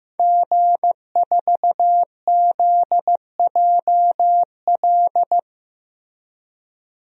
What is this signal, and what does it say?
Morse code, heard as a keyed tone, about 15 wpm, 700 Hz. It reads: G4ZJL
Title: DXCC-100 Call Signs - 5 Characters - 15wpm